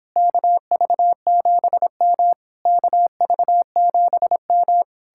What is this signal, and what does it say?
Morse code, heard as a keyed tone, about 29 wpm, 700 Hz. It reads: KV7M KV7M